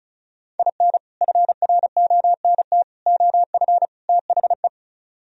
Morse code, heard as a keyed tone, about 45 wpm, 700 Hz. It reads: IN FRONT OF THE